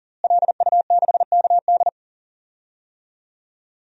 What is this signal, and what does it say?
Morse code, heard as a keyed tone, about 40 wpm, 700 Hz. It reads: LU6KD
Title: DXCC-100 Call Signs - 5 Characters - Repeated 2x - 40wpm